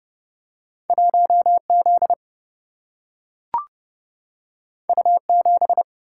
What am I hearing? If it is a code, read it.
Morse code, heard as a keyed tone, about 30 wpm, 700 Hz. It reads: 1Z U7
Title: Single-Letter Single-Number 30wpm